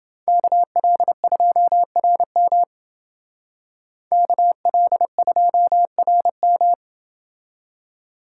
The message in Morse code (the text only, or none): KL2RM KL2RM